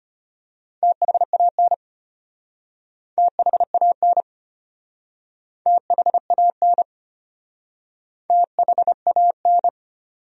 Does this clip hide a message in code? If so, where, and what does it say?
Yes: Morse code from 0.8 to 9.7 s, heard as a keyed tone, about 35 wpm, 700 Hz. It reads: THAN THAN THAN THAN